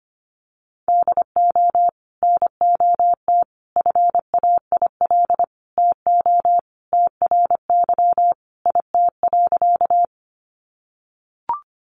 Morse code, heard as a keyed tone, about 25 wpm, 700 Hz. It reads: DO NOT FAIL TO TRY IT.
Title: Sentences from Top 1100 Words 25wpm